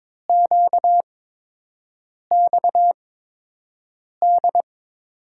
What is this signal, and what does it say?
Morse code, heard as a keyed tone, about 21 wpm, spaced as for 6 wpm, 700 Hz. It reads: QXD
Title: USA Call Signs - 5 Characters - 8x Character Spacing - 22wpm